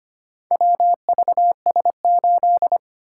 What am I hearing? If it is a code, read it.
Morse code, heard as a keyed tone, about 25 wpm, 700 Hz. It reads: WVS8
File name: practice / 25wpm / Four Letter-Number Combinations - 25wpm